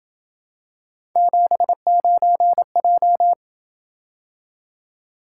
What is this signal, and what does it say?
Morse code, heard as a keyed tone, about 27 wpm, 700 Hz. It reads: 79J